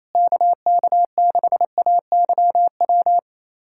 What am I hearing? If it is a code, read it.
Morse code, heard as a keyed tone, about 28 wpm, 700 Hz. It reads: KK6AYW
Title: USA Callsign DE Callsign - 6 Characters - 28wpm